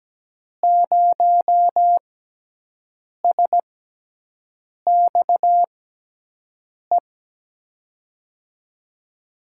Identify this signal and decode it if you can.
Morse code, heard as a keyed tone, about 17 wpm, spaced as for 6 wpm, 700 Hz. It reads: 0SXE
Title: USA Call Signs - 6x Character Spacing - 17wpm